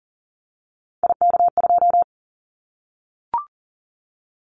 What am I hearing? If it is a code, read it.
Morse code, heard as a keyed tone, about 40 wpm, 700 Hz. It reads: IK2
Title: Callsign Prefixes for Top 25 DXCC - Repeated 3x - 40wpm